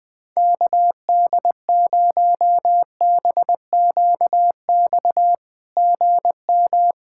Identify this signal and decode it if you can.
Morse code, heard as a keyed tone, about 20 wpm, 700 Hz. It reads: KD0BQX GM
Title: SOTA QSO Practice 20wpm